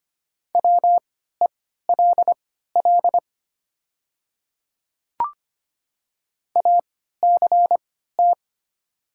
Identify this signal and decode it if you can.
Morse code, heard as a keyed tone, about 25 wpm, spaced as for 14 wpm, 700 Hz. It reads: WELL ACT
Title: 3x Character Spacing - Top 200 Words - 25wpm